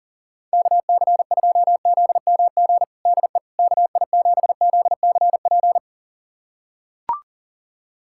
Morse code, heard as a keyed tone, about 40 wpm, 700 Hz. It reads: KC2ZMG DE KI7ZCP
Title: USA Callsign DE Callsign - 6 Characters - 40wpm